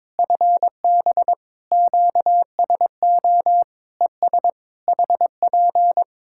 Morse code, heard as a keyed tone, about 22 wpm, 700 Hz. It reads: FB QSO ES HP
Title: Practice QSO - Exchange 4 - Ending - 22wpm